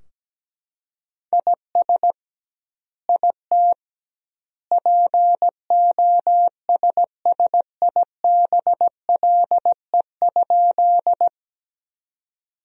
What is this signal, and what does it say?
Morse code, heard as a keyed tone, about 17 wpm, 700 Hz. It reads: IS IT POSSIBLE?
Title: 2x Word-Spacing - Sentences from Top 500 Words - 17wpm